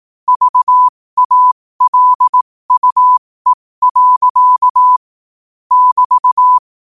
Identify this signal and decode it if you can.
Morse code, heard as a keyed tone, about 18 wpm, spaced as for 15 wpm, 1000 Hz. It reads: VALUE. <BT>
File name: qotdfiles / 15wpm